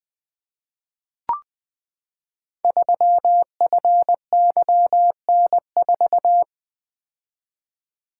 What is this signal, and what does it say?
Morse code, heard as a keyed tone, about 20 wpm, 700 Hz. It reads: 3FYN4
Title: Code Groups - 5 Letter:Numbers - 20wpm